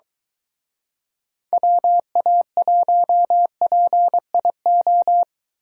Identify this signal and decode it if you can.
Morse code, heard as a keyed tone, about 22 wpm, 700 Hz. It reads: WA1PIO